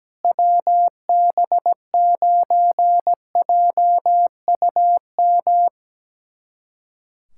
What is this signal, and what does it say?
Morse code, heard as a keyed tone, about 17 wpm, 700 Hz. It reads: WB9JUM
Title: USA Callsign DE Callsign 17wpm